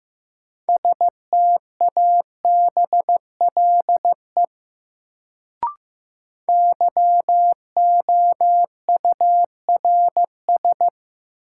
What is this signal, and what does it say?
Morse code, heard as a keyed tone, about 15 wpm, 700 Hz. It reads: STABLE YOURS
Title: Sentences from Top 1500 Words - New Word Drill - 15wpm